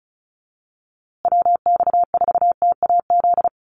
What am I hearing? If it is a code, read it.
Morse code, heard as a keyed tone, about 34 wpm, 700 Hz. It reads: WX4TAZ
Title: USA Call Signs Repeated 2x 35wpm